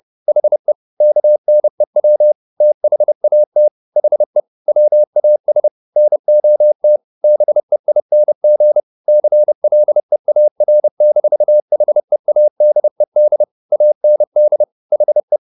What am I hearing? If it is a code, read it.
Morse code, heard as a keyed tone, about 30 wpm, 600 Hz. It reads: HE KNEW THAT HE WAS NOT BEING CLEAR-HEADED AND HE